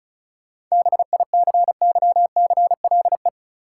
Morse code, heard as a keyed tone, about 36 wpm, 700 Hz. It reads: BICYCLE
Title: Seven Letter Words - 35wpm